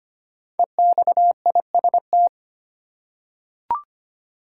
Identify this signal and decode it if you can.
Morse code, heard as a keyed tone, about 25 wpm, 700 Hz. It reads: EXIST
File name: 5 Letter Words - First 3 Letters Given - 25wpm